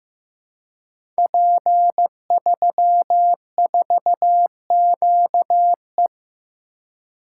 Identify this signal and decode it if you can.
Morse code, heard as a keyed tone, about 15 wpm, 700 Hz. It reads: P34QE
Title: Code Groups - 5 Letter:Numbers - 15wpm